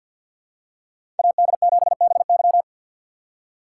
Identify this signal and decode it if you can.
Morse code, heard as a keyed tone, about 53 wpm, 700 Hz. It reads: AD7BY
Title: USA Call Signs - 5 Characters - 3x Character Spacing - 50wpm